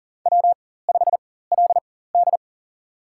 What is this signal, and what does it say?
Morse code, heard as a keyed tone, about 40 wpm, spaced as for 19 wpm, 700 Hz. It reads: W5LD
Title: USA Call Signs - 4 Characters - 4x Character Spacing - 40wpm